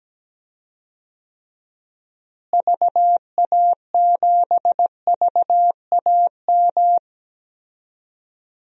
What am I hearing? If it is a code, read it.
Morse code, heard as a keyed tone, about 17 wpm, 700 Hz. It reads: VA7VAM
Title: DXCC-100 Call Signs - 6 Characters - Repeated 2x - 17wpm